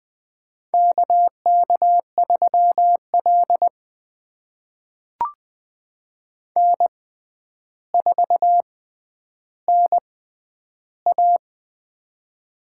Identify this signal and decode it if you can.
Morse code, heard as a keyed tone, about 21 wpm, spaced as for 10 wpm, 700 Hz. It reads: KK3L N4NA
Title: USA Call Signs - 4 Characters - 6x Character Spacing - 20wpm